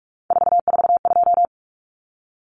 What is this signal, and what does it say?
Morse code, heard as a keyed tone, about 44 wpm, 700 Hz. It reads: 442
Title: Numbers - 3-Digit 45wpm